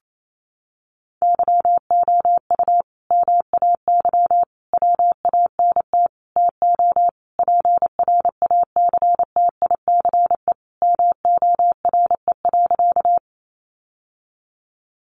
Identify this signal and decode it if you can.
Morse code, heard as a keyed tone, about 28 wpm, 700 Hz. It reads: YOU MAY WANT TO PRACTICE MORE.
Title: Sentences from Top 700 Words 28wpm